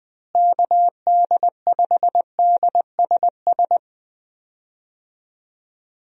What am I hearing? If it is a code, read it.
Morse code, heard as a keyed tone, about 20 wpm, 700 Hz. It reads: KD5DSS
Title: USA Call Signs - 6 Characters - Repeated 2x - 20wpm